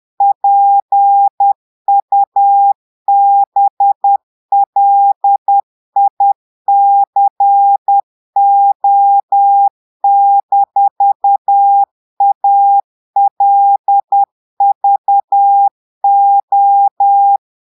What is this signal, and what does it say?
Morse code, heard as a keyed tone, about 10 wpm, 800 Hz. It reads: PUBLICO-ALVO
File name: portugais texte en clair 2 vitesse 10 mots minutes